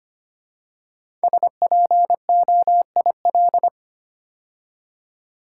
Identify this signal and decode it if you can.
Morse code, heard as a keyed tone, about 25 wpm, 700 Hz. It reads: SPOIL
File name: 5 Letter Words - First 3 Letters Given - 25wpm